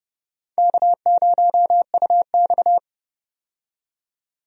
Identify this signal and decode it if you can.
Morse code, heard as a keyed tone, about 30 wpm, 700 Hz. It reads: K0UX
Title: USA Call Signs - 4 Characters - 30wpm